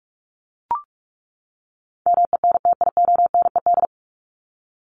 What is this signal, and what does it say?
Morse code, heard as a keyed tone, about 44 wpm, 700 Hz. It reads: MENTIONED
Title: ED Suffix - Top 100 Words with Base Words 45wpm